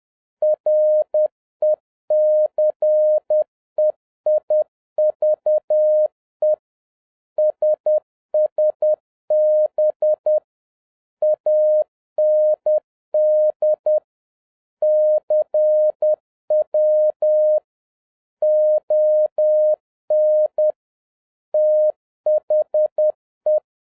Morse code, heard as a keyed tone, about 10 wpm, 610 Hz. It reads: RECEIVE SSB AND CW ON THE
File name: 2011-10-mp3 / n11-MEETING WITH GOOD FRIENDS 10wpm